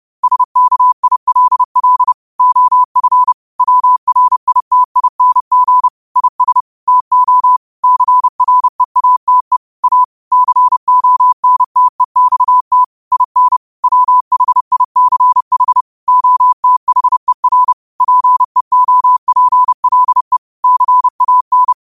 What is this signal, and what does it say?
Morse code, heard as a keyed tone, about 30 wpm, 1000 Hz. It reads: SKILL OF WRITING IS TO CREATE A CONTEXT IN WHICH OTHER PEOPLE CAN
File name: qotdfiles / 30wpm